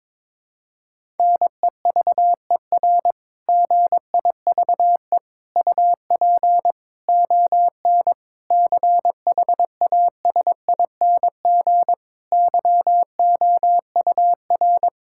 Morse code, heard as a keyed tone, about 21 wpm, 700 Hz. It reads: NEVER GIVE UP ON CHASING YOUR
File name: Difficult Sentences with Hints - 22wpm